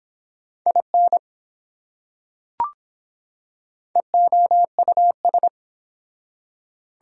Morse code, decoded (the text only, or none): IN EOUS